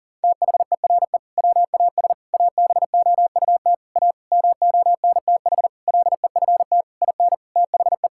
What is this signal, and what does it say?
Morse code, heard as a keyed tone, about 40 wpm, 700 Hz. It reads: THERE WAS ABOUT A MONTH LEFT IN THE
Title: Sentences from Top 1000 Words - Review Part 5 - 40wpm